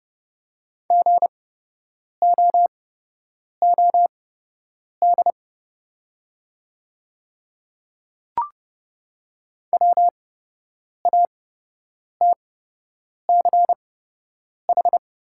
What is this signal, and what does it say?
Morse code, heard as a keyed tone, about 30 wpm, spaced as for 8 wpm, 700 Hz. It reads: GOOD WATCH
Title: 8x Character Spacing - Top 500 Words - 30wpm